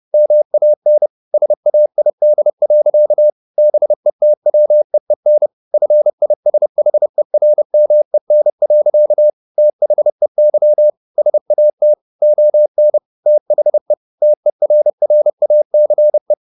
Morse code, heard as a keyed tone, about 30 wpm, 600 Hz. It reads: MAN SAID. BETWEEN FISHERMEN. THEY SAT ON THE TERRACE
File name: the old man and the sea / the old man and the sea 1 30wpm